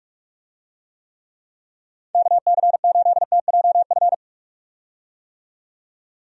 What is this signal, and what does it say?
Morse code, heard as a keyed tone, about 43 wpm, 700 Hz. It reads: KC8TJF
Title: USA Call Signs - 6 Characters - 45wpm